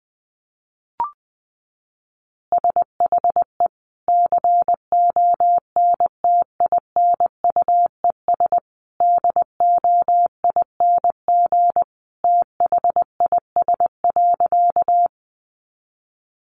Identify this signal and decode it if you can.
Morse code, heard as a keyed tone, about 20 wpm, 700 Hz. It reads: SHE CONTINUES DOING THIS.
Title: Easy Phrases 20wpm